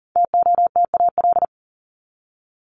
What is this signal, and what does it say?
Morse code, heard as a keyed tone, about 40 wpm, 700 Hz. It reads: TOTAL
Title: LY Suffix - Top 100 Words with Base Words 40wpm